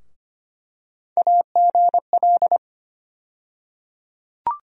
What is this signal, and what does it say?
Morse code, heard as a keyed tone, about 25 wpm, 700 Hz. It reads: AGL
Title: USA Call Sign Postfixes - 3 Letters 25wpm